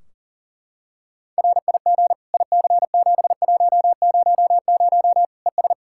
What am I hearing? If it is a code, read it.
Morse code, heard as a keyed tone, about 40 wpm, 700 Hz. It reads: RIG IC7100 ES
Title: K1USN SST QSO Practice 40wpm